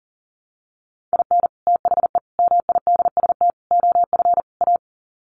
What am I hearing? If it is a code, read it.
Morse code, heard as a keyed tone, about 40 wpm, 700 Hz. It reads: IN THE MIDST OF A
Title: Sets of 5 Words 40wpm